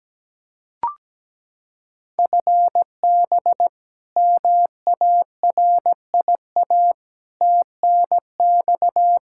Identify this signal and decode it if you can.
Morse code, heard as a keyed tone, about 17 wpm, 700 Hz. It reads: FB MARIA TNX
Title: Practice QSO - Exchange 3 - More Info - 17wpm